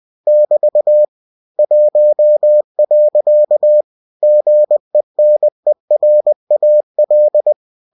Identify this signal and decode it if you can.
Morse code, heard as a keyed tone, about 20 wpm, 600 Hz. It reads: <BT> 1. GENERAL
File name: cw-books / winnie-the-pooh / winnie-20wpm-page-0063